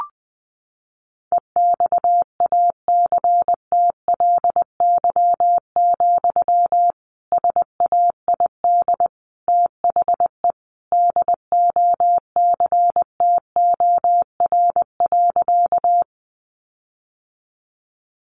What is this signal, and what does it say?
Morse code, heard as a keyed tone, about 20 wpm, 700 Hz. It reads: EXACTLY, SAID THE DOCTOR.
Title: Sentences from Top 1500 Words 20wpm